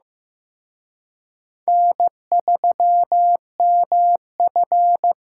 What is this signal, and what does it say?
Morse code, heard as a keyed tone, about 18 wpm, 700 Hz. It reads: N3MF
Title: Speed Racing - USA 1x2 Call Signs 15wpm